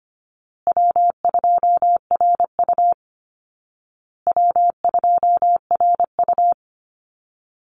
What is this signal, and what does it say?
Morse code, heard as a keyed tone, about 25 wpm, 700 Hz. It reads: W2RU W2RU
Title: Easy Call Signs Repeated 3x - 25wpm